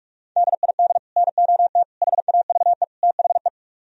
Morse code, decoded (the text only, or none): DID NOT HAVE THE